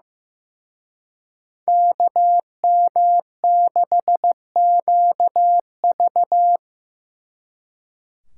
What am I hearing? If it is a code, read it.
Morse code, heard as a keyed tone, about 18 wpm, 700 Hz. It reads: KM6QV
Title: Speed Racing - USA 2x2 Call Signs 15wpm